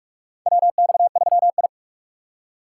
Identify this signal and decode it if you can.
Morse code, heard as a keyed tone, about 48 wpm, 700 Hz. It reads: WX3I
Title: USA Call Signs - 4 Characters - 6x Character Spacing - 45wpm